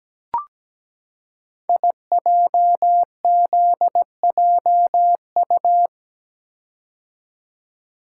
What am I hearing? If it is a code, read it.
Morse code, heard as a keyed tone, about 17 wpm, 700 Hz. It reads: IJZJU
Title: Code Groups - 5 Letter:Numbers - 17wpm